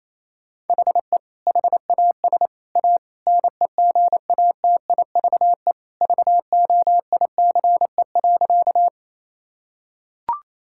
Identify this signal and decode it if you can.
Morse code, heard as a keyed tone, about 28 wpm, 700 Hz. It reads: HE HAS A NEGATIVE VOICE.
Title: Sentences from Top 1000 Words - Review Part 6 - 28wpm